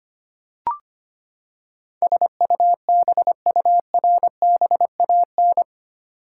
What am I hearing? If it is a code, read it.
Morse code, heard as a keyed tone, about 25 wpm, 700 Hz. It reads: SUBURBAN
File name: Eight Letter Words - 25wpm